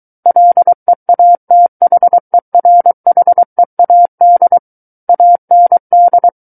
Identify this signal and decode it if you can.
Morse code, heard as a keyed tone, about 23 wpm, 700 Hz. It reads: LEATHERHEAD AND
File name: Book One / 14-Chapter 14 - 23 WPM